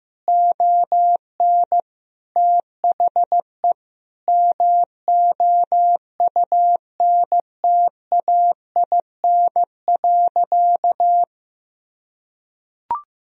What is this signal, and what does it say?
Morse code, heard as a keyed tone, about 15 wpm, 700 Hz. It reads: ON THE MOUNTAIN.
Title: Sentences from Top 1000 Words - Review Part 1 - 15wpm